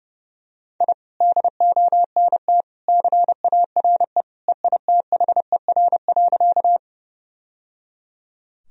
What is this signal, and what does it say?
Morse code, heard as a keyed tone, about 30 wpm, 700 Hz. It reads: I DONT CARE EITHER.
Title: Sentences from Top 1000 Words - Review Part 4 - 30wpm